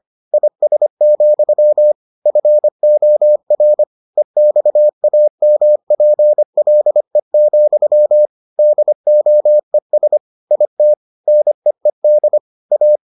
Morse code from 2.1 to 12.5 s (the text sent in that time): FOR EXAMPLE, DOES IT NEED